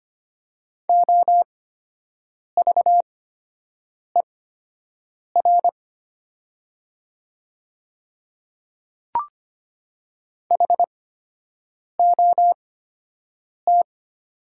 Morse code, heard as a keyed tone, about 25 wpm, spaced as for 7 wpm, 700 Hz. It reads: OVER HOT
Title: 8x Character Spacing - Top 100 Words - 25wpm